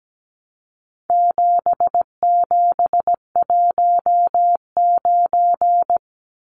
Morse code, heard as a keyed tone, about 17 wpm, 700 Hz. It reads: 7719
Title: Numbers - 4-Digit 17wpm